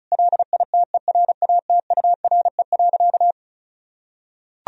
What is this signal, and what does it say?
Morse code, heard as a keyed tone, about 34 wpm, 700 Hz. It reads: LITERATURE.
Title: Sentences from Top 1000 Words - Review Part 4 - 35wpm